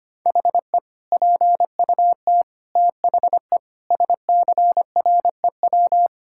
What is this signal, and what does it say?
Morse code, heard as a keyed tone, about 25 wpm, 700 Hz. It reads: HE PUT THE SCREW
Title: Sentences from Top 1600 Words 25wpm